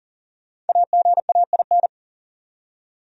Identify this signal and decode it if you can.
Morse code, heard as a keyed tone, about 52 wpm, 700 Hz. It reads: AGAIN